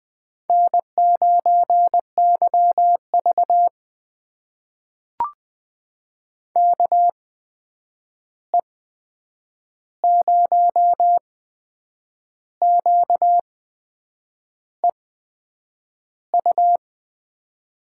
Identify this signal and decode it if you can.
Morse code, heard as a keyed tone, about 20 wpm, spaced as for 8 wpm, 700 Hz. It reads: N9YV KE0QEU
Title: USA Call Signs - 8x Character Spacing - 20wpm